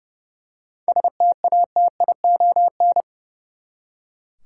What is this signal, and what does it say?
Morse code, heard as a keyed tone, about 30 wpm, 700 Hz. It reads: STATION
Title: Seven Letter Words - 30wpm